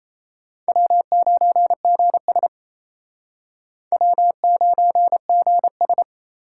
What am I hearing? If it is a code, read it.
Morse code, heard as a keyed tone, about 36 wpm, 700 Hz. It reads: W9GS W9GS